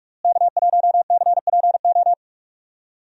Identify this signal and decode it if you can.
Morse code, heard as a keyed tone, about 47 wpm, 700 Hz. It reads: K1CPO
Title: USA Call Signs - 5 Characters - Repeated 2x - 45wpm